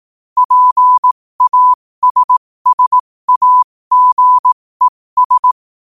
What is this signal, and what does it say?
Morse code, heard as a keyed tone, about 18 wpm, spaced as for 15 wpm, 1000 Hz. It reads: PASSAGES
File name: qotdfiles / 15wpm